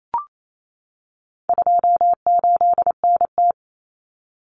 Morse code, heard as a keyed tone, about 28 wpm, 700 Hz. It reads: 28NT
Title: Common Cut Numbers - 4-Digit 28wpm